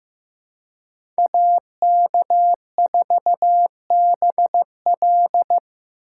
Morse code, heard as a keyed tone, about 16 wpm, 700 Hz. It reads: AK4BL
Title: USA Call Signs - 2x Character Spacing - 15wpm